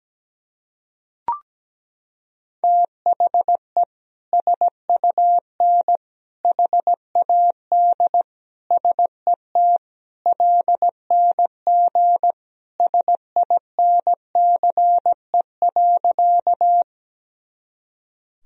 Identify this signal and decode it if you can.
Morse code, heard as a keyed tone, about 17 wpm, 700 Hz. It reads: THE SUN HAD SET LNG SINCE.
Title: Sentences from Top 300 Words and Abbreviations 17wpm